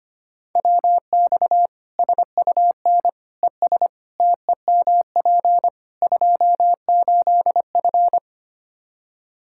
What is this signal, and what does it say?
Morse code, heard as a keyed tone, about 25 wpm, 700 Hz. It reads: WX SUN ES TEMP 28F
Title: Practice QSO - Exchange 2 - Info - 25wpm